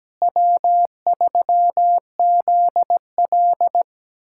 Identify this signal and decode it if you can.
Morse code, heard as a keyed tone, about 17 wpm, 700 Hz. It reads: W3ZL
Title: USA Callsign DE Callsign - 4 Characters - 17wpm